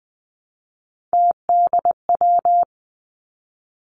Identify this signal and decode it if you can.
Morse code, heard as a keyed tone, about 20 wpm, 700 Hz. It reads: TDW
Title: Three Challenging Letter-Number Combinations 20wpm